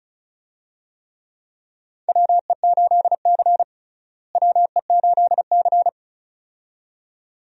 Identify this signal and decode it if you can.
Morse code, heard as a keyed tone, about 36 wpm, 700 Hz. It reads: WE8C WE8C
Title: USA Call Signs - 4 Characters - Repeated 2x - 35wpm